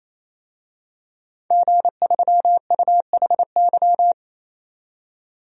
Morse code, heard as a keyed tone, about 28 wpm, 700 Hz. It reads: G3UHY
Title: DXCC-100 Call Signs - 5 Characters - 28wpm